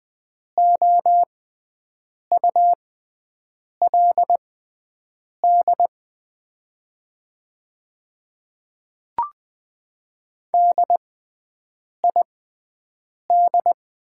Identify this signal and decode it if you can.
Morse code, heard as a keyed tone, about 20 wpm, spaced as for 7 wpm, 700 Hz. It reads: OULD DID
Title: 6x Character Spacing - Top 300 Words - 20wpm